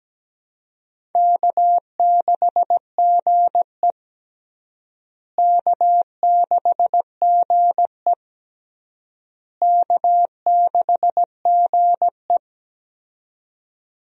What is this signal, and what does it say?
Morse code, heard as a keyed tone, about 17 wpm, 700 Hz. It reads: K6GE K6GE K6GE
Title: USA Call Signs - 4 Characters - Repeated 3x - 17wpm